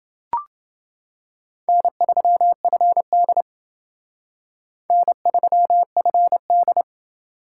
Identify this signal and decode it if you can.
Morse code, heard as a keyed tone, about 28 wpm, 700 Hz. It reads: N3FD N3FD